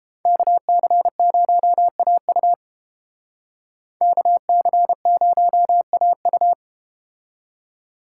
Morse code, heard as a keyed tone, about 31 wpm, 700 Hz. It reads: KC0AU KC0AU